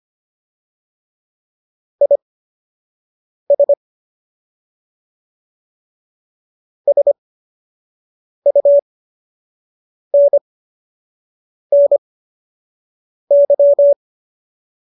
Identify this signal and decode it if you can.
Morse code, heard as a keyed tone, about 25 wpm, spaced as for 6 wpm, 570 Hz. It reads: IS SUNNY
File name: qso002 06